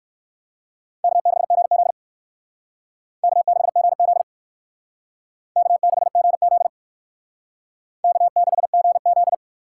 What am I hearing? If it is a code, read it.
Morse code, heard as a keyed tone, about 56 wpm, 700 Hz. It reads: K6GZ K6GZ K6GZ K6GZ
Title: Speed Racing - USA Call Signs 45wpm